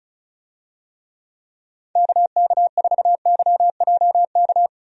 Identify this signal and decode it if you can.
Morse code, heard as a keyed tone, about 34 wpm, 700 Hz. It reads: KK4YJK
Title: USA Call Signs - 6 Characters - Repeated 2x - 35wpm